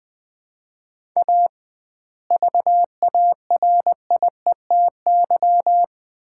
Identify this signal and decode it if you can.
Morse code, heard as a keyed tone, about 21 wpm, 700 Hz. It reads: A VARIETY
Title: 2x Word-Spacing - Sets of 3 Words - 20wpm